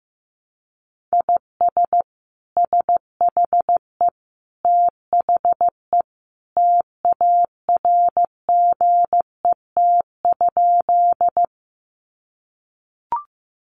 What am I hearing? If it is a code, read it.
Morse code, heard as a keyed tone, about 15 wpm, 700 Hz. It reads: IS SHE THE TARGET?
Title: Sentences from Top 1200 Words 15wpm